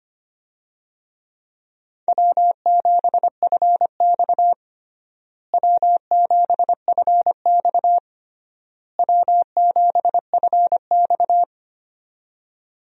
Morse code, heard as a keyed tone, about 25 wpm, 700 Hz. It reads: W7FX W7FX W7FX
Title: USA Call Signs - 4 Characters - Repeated 3x - 25wpm